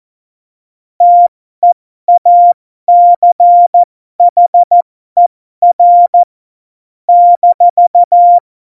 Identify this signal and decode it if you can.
Morse code, heard as a keyed tone, about 14 wpm, spaced as for 12 wpm, 700 Hz. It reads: TEACHER -
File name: Morsekurs / Witze